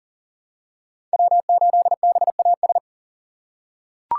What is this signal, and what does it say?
Morse code, heard as a keyed tone, about 40 wpm, 700 Hz. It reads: W8BAS
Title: USA Call Signs - 2x Character Spacing - 40wpm